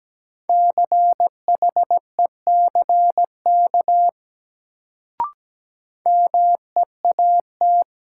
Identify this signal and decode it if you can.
Morse code, heard as a keyed tone, about 17 wpm, 700 Hz. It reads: CHECK MEAT
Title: Sentences from Top 1000 Words - Review Part 3 - New Word Drill - 17wpm